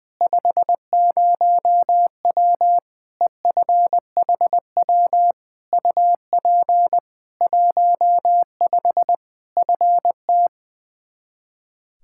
Morse code, heard as a keyed tone, about 20 wpm, 700 Hz. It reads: 50W EFHW UP 15 FT
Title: K1USN SST QSO Practice 20wpm